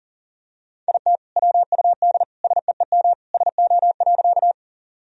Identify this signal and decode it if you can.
Morse code, heard as a keyed tone, about 40 wpm, 700 Hz. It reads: IT WUD SEEM SO.
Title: Sentences from Top 300 Words and Abbreviations 40wpm